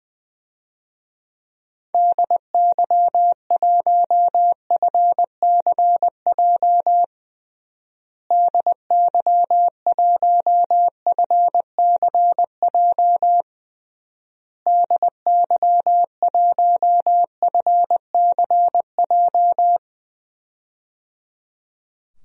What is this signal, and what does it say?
Morse code, heard as a keyed tone, about 20 wpm, 700 Hz. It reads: DY1FCJ DY1FCJ DY1FCJ
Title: DXCC-100 Call Signs - 6 Characters - Repeated 3x - 20wpm